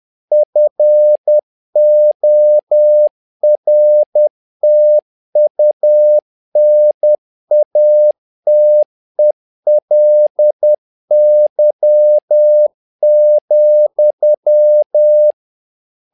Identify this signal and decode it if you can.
Morse code, heard as a keyed tone, about 10 wpm, 600 Hz. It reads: FORTUNATELY,
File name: learning center / tower trouble / tower trouble 10wpm